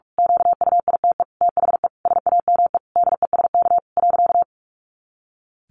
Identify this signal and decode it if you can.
Morse code, heard as a keyed tone, about 44 wpm, 700 Hz. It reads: QUITE THE SAME DESK .
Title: A Sound Of Thunder 45wpm